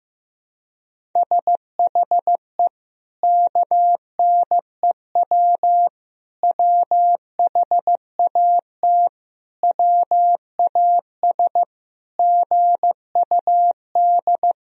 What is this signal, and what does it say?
Morse code, heard as a keyed tone, about 15 wpm, 700 Hz. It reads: SHE KNEW WHAT WAS GUD
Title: Sentences from Top 400 Words and Abbreviations 15wpm